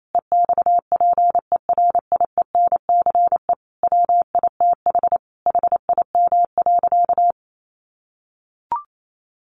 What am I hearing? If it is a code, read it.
Morse code, heard as a keyed tone, about 28 wpm, 700 Hz. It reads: EXPERIENCE WITH HIM.
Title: Sentences from Top 1400 Words 28wpm